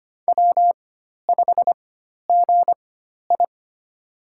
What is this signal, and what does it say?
Morse code, heard as a keyed tone, about 25 wpm, spaced as for 12 wpm, 700 Hz. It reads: W5GI
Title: USA Call Signs - 4 Characters - 4x Character Spacing - 25wpm